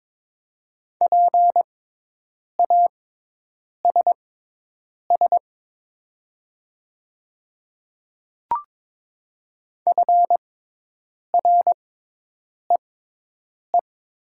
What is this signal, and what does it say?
Morse code, heard as a keyed tone, about 22 wpm, spaced as for 8 wpm, 700 Hz. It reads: PASS FREE
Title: 6x Character Spacing - Top 500 Words - 22wpm